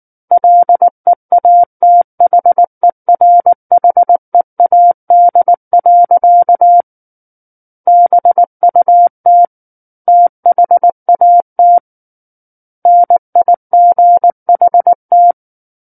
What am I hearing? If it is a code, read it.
Morse code, heard as a keyed tone, about 19 wpm, 700 Hz. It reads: LEATHERHEAD. BUT THAT NIGHT
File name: Book One / 10-Chapter 10 - 19 WPM